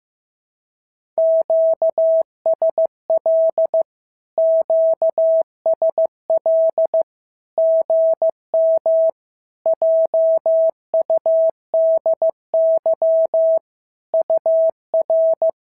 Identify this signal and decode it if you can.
Morse code, heard as a keyed tone, about 15 wpm, 650 Hz. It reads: QSL QSL GM JUDY UR
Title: SOTA QSO Practice 15wpm